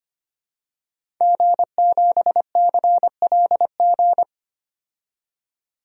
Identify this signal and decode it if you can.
Morse code, heard as a keyed tone, about 25 wpm, 700 Hz. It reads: G7CLG
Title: DXCC-100 Call Signs - 5 Characters - 25wpm